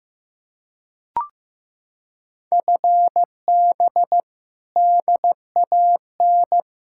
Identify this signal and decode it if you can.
Morse code, heard as a keyed tone, about 15 wpm, 700 Hz. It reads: FB DAN
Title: Practice QSO - Exchange 3 - More Info - 15wpm